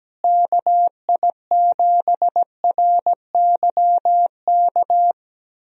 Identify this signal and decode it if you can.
Morse code, heard as a keyed tone, about 17 wpm, 700 Hz. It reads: KI7RYK
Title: USA Call Signs - 6 Characters - Repeated 2x - 17wpm